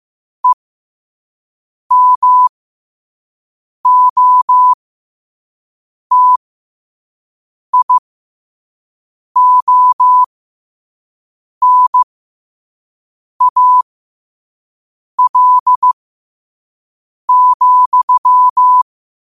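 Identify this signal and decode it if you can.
Morse code, heard as a keyed tone, about 15 wpm, spaced as for 5 wpm, 1000 Hz. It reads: EMOTIONAL,
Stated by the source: Morse code practice at 5 wpm.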